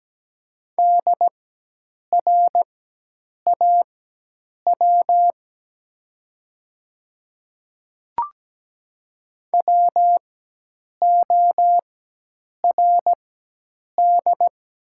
Morse code, heard as a keyed tone, about 17 wpm, spaced as for 8 wpm, 700 Hz. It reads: DRAW WORD
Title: 4x Character Spacing - Top 500 Words - 17wpm